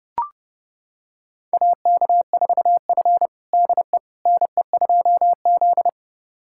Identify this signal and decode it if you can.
Morse code, heard as a keyed tone, about 32 wpm, 700 Hz. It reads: AK4F DE NE2Z
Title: USA Callsign DE Callsign - 4 Characters - 30wpm